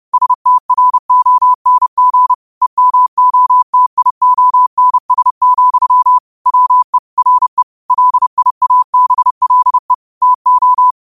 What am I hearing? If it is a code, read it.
Morse code, heard as a keyed tone, about 30 wpm, 1000 Hz. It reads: STRONG EMOTIONS, WE RE LIABLE TO
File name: qotdfiles / 30wpm